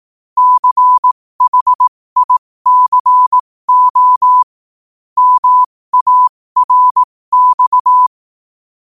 Morse code, heard as a keyed tone, about 18 wpm, spaced as for 15 wpm, 1000 Hz. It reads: CHICO MARX
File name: qotdfiles / 15wpm